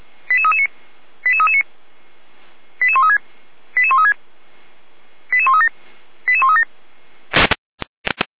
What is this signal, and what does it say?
Five-tone selective call (ZVEI), heard as quick runs of five tones, tones of about 70 ms. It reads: 88339 88339 88136 88136 88137 88137
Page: /455/SDR/ZVEI/